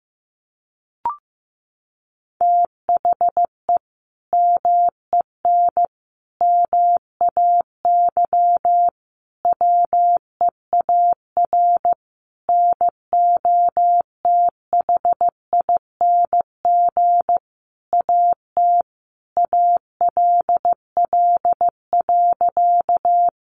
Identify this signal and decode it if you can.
Morse code, heard as a keyed tone, about 15 wpm, 700 Hz. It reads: THE MEN MAY WEAR NOTHING AT ALL.
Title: Sentences from Top 1500 Words 15wpm